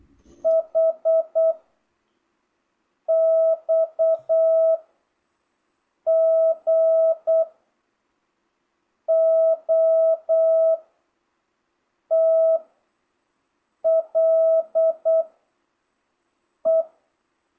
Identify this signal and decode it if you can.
Morse code, heard as a keyed tone, about 8 wpm, 660 Hz. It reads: H X G O T L E